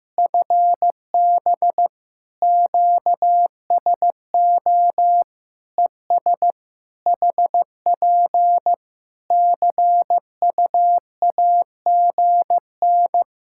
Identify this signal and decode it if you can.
Morse code, heard as a keyed tone, about 15 wpm, 700 Hz. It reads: FB QSO ES HP CUAGN
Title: K1USN SST QSO Practice 15wpm